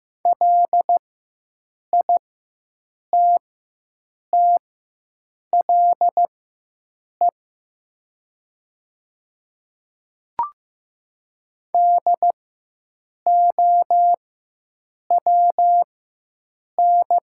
Morse code, heard as a keyed tone, about 15 wpm, spaced as for 7 wpm, 700 Hz. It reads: LITTLE DOWN
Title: 4x Character Spacing - Top 300 Words - 15wpm